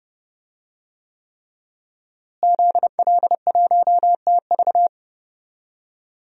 Morse code, heard as a keyed tone, about 30 wpm, 700 Hz. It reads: ZL1TV
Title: DXCC-100 Call Signs - 5 Characters - Repeated 3x - 30wpm